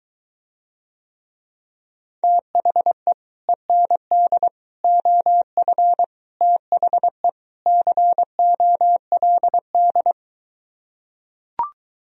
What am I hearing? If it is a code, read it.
Morse code, heard as a keyed tone, about 23 wpm, 700 Hz. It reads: THE END OF THE COLD